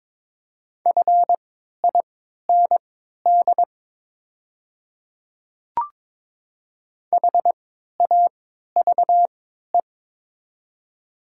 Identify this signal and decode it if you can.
Morse code, heard as a keyed tone, about 22 wpm, spaced as for 12 wpm, 700 Hz. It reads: FIND HAVE
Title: 3x Character Spacing - Top 100 Words - 22wpm